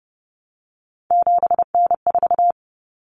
Morse code, heard as a keyed tone, about 30 wpm, 700 Hz. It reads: 7N4
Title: Common Cut Numbers - 3-Digit 30wpm